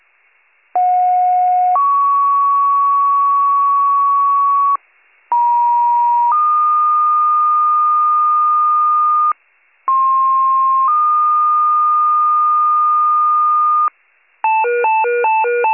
Two-tone page, heard as a one-second tone, then a three-second tone, about 730 then 1120 Hz, about 950 then 1220 Hz, about 1060 then 1220 Hz. It is heard from 0.7 to 13.9 s.